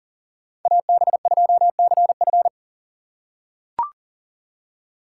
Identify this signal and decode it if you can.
Morse code, heard as a keyed tone, about 40 wpm, 700 Hz. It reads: AB2CF
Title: USA Call Signs - 5 Characters - 2x Character Spacing - 40wpm